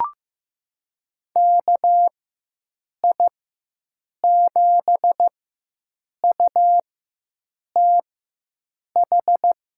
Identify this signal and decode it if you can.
Morse code, heard as a keyed tone, about 15 wpm, spaced as for 7 wpm, 700 Hz. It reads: KI7UTH
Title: USA Call Signs - 6 Characters - 4x Character Spacing - 15wpm